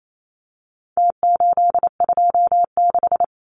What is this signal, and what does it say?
Morse code, heard as a keyed tone, about 27 wpm, 700 Hz. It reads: T826
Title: Numbers - Contest Style - 4-Digit 28wpm